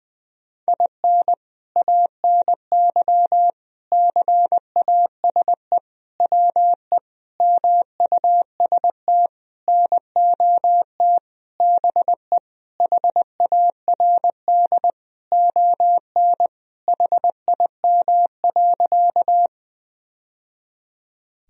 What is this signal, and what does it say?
Morse code, heard as a keyed tone, about 21 wpm, 700 Hz. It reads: IN ANY CASE WE MUST NOT BE HARD ON HIM.
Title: Sentences from Top 1000 Words - Review Part 1 - 20wpm